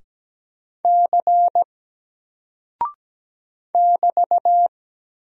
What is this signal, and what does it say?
Morse code, heard as a keyed tone, about 17 wpm, 700 Hz. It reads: C <BT>
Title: Single Character (Letter:Number:Punctuation:Prosign) Alt 17wpm